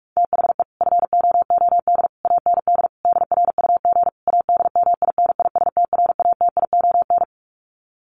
Morse code, heard as a keyed tone, about 44 wpm, 700 Hz. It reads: THE FOOD AND DRUG ADMINISTRATION
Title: Sets of 5 Words 45wpm